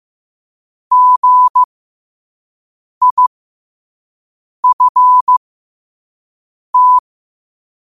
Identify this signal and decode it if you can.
Morse code, heard as a keyed tone, about 15 wpm, spaced as for 5 wpm, 1000 Hz. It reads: GIFT
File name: qotdfiles / 5wpm